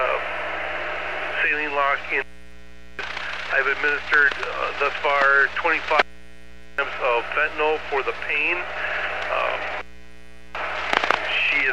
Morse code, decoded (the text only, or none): none